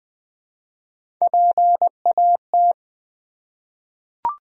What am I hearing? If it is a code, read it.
Morse code, heard as a keyed tone, about 20 wpm, 700 Hz. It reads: PAT
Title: USA Common Names - Max 5 Letters - 20wpm